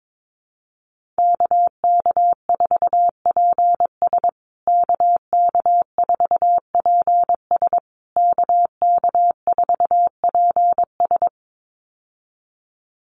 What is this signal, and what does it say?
Morse code, heard as a keyed tone, about 22 wpm, 700 Hz. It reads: KK4PS KK4PS KK4PS
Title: USA Call Signs Repeated 3x 22wpm